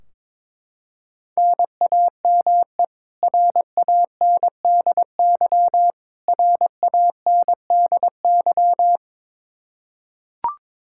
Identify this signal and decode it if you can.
Morse code, heard as a keyed tone, about 22 wpm, 700 Hz. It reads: NAME RANDY RANDY
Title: Practice QSO - Exchange 1 - Beginning - 22wpm